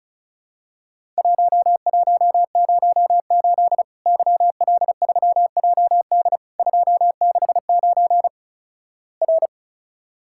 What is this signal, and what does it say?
Morse code, heard as a keyed tone, about 36 wpm, 700 Hz. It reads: 1108 YL3JD 269 R
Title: Worked All Europe DX Contest - QTC Exchange - 35wpm